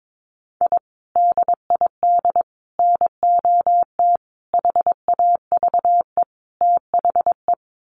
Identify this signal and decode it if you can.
Morse code, heard as a keyed tone, about 22 wpm, 700 Hz. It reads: I DID NOT HAVE THE
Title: Sets of 5 Words Encore 22wpm